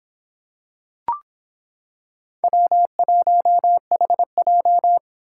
Morse code, heard as a keyed tone, about 26 wpm, 700 Hz. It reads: W1HJ